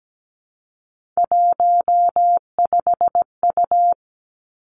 Morse code, heard as a keyed tone, about 17 wpm, 700 Hz. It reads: 15U
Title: Three Challenging Letter-Number Combinations 17wpm